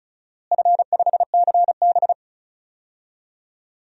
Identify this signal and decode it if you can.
Morse code, heard as a keyed tone, about 36 wpm, 700 Hz. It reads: F5CB
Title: DXCC-100 Call Signs - 4 Characters - Repeated 2x - 35wpm